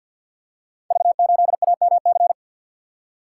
Morse code, heard as a keyed tone, about 53 wpm, 700 Hz. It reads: V8AMC
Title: Code Groups - 5 Letter:Numbers - 50wpm